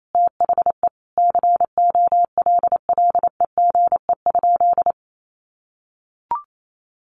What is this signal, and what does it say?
Morse code, heard as a keyed tone, about 28 wpm, 700 Hz. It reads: THE COLLEGE?
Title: Sentences from Top 1300 Words 28wpm